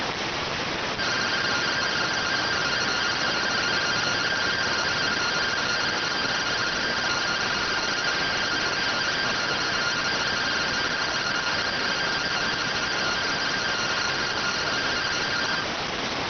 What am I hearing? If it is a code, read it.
Two-tone teleprinter signal, about 170 Hz shift.
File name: findings / MR AXIAL fr1 120.gen-14 rx